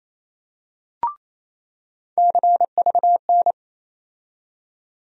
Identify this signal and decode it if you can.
Morse code, heard as a keyed tone, about 28 wpm, 700 Hz. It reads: CVN
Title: USA Call Sign Postfixes - 3 Letters 28wpm